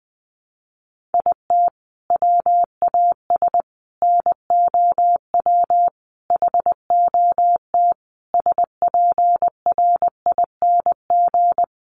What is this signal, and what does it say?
Morse code, heard as a keyed tone, about 20 wpm, 700 Hz. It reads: IT WAS NOW HOT SPRING
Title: Sentences from Top 1200 Words 20wpm